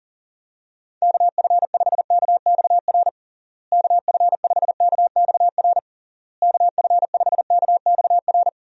Morse code, heard as a keyed tone, about 40 wpm, 700 Hz. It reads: KF5KXR KF5KXR KF5KXR
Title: USA Call Signs - 6 Characters - Repeated 3x - 40wpm